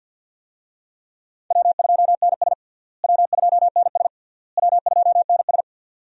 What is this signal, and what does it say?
Morse code, heard as a keyed tone, about 50 wpm, 700 Hz. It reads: W2NS W2NS W2NS
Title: USA Call Signs - 4 Characters - Repeated 3x - 50wpm